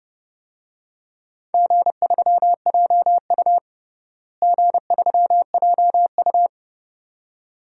Morse code, heard as a keyed tone, about 30 wpm, 700 Hz. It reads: G3JU G3JU
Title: DXCC-100 Call Signs - 4 Characters - Repeated 2x - 30wpm